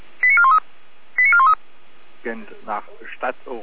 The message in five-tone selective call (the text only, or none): 89413 89413